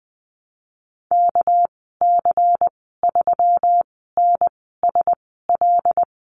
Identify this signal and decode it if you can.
Morse code, heard as a keyed tone, about 20 wpm, spaced as for 14 wpm, 700 Hz. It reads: KC3NSL
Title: 245 Lesson 39g - Letter z - Call Signs 20wpm